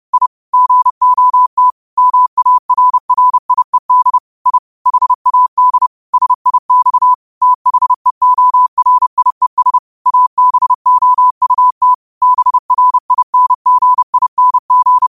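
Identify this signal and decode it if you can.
Morse code, heard as a keyed tone, about 30 wpm, 1000 Hz. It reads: I GOT MARRIED I HAD SIX THEORIES ABOUT BRINGING
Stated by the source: Morse code quote of the day sent at 30 wpm.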